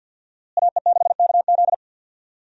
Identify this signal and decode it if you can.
Morse code, heard as a keyed tone, about 52 wpm, 700 Hz. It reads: AE6KZ
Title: USA Call Signs - 5 Characters - Repeated 2x - 50wpm